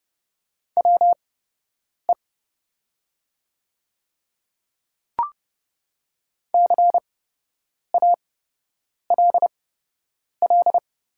Morse code, heard as a keyed tone, about 30 wpm, spaced as for 8 wpm, 700 Hz. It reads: WE CALL
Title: 8x Character Spacing - Top 300 Words - 30wpm